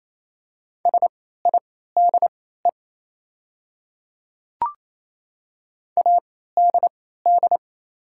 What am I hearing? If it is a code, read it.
Morse code, heard as a keyed tone, about 28 wpm, spaced as for 16 wpm, 700 Hz. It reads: SIDE ADD
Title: 3x Character Spacing - Top 200 Words - 28wpm